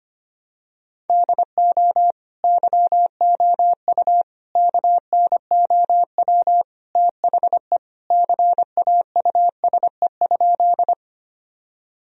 Morse code, heard as a keyed tone, about 25 wpm, 700 Hz. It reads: DO YOU KNOW THE CAUSE?
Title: Sentences from Top 1000 Words - Review Part 1 - 25wpm